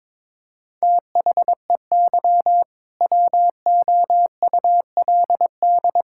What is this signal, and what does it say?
Morse code, heard as a keyed tone, about 22 wpm, 700 Hz. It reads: THEY WOULD
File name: practice / 22wpm / Sentences with 2-Word Phrases - 22wpm